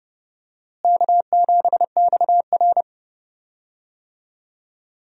Morse code, heard as a keyed tone, about 32 wpm, 700 Hz. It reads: K7XR
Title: USA Call Signs - 4 Characters - Repeated 2x - 30wpm